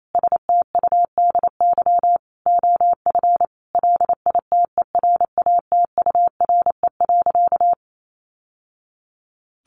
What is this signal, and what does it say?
Morse code, heard as a keyed tone, about 28 wpm, 690 Hz. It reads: STUDY OF LITERATURE.
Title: Sentences from Top 900 Words 28wpm